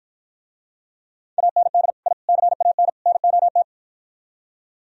Morse code, heard as a keyed tone, about 54 wpm, 700 Hz. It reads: AND I CAN NOT